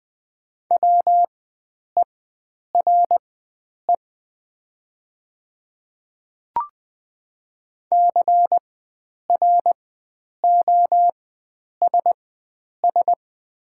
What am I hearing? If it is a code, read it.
Morse code, heard as a keyed tone, about 20 wpm, spaced as for 9 wpm, 700 Hz. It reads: WERE CROSS
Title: 4x Character Spacing - Top 300 Words - 20wpm